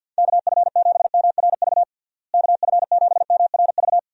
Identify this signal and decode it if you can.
Morse code, heard as a keyed tone, about 46 wpm, 700 Hz. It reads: KF7MRV KF7MRV
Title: USA Call Signs - 6 Characters - Repeated 2x - 50wpm